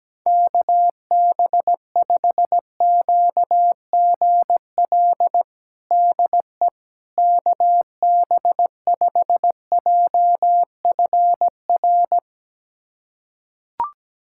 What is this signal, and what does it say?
Morse code, heard as a keyed tone, about 17 wpm, 700 Hz. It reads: KB5QGL DE KB5JFR
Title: USA Callsign DE Callsign - 6 Characters - 17wpm